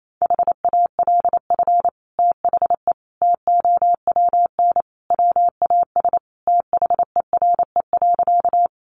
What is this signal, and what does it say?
Morse code, heard as a keyed tone, about 28 wpm, 700 Hz. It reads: HALF THE TOWN WAS THERE.
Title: Sentences from Top 500 Words 28wpm